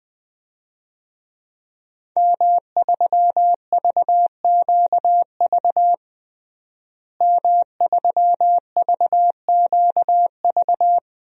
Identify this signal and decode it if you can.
Morse code, heard as a keyed tone, about 20 wpm, 700 Hz. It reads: M3VQV M3VQV
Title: DXCC-100 Call Signs - 5 Characters - Repeated 3x - 20wpm